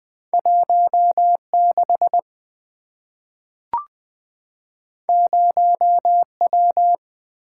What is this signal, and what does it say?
Morse code, heard as a keyed tone, about 20 wpm, 690 Hz. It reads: 16 0W
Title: Two Letter-Number Combinations - 20wpm